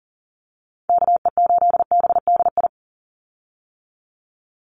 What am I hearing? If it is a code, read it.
Morse code, heard as a keyed tone, about 40 wpm, 700 Hz. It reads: KE8BDI
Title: USA Call Signs 40wpm